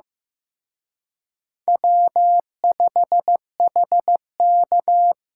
Morse code, heard as a keyed tone, about 18 wpm, 700 Hz. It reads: W5HK
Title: Speed Racing - USA 1x2 Call Signs 15wpm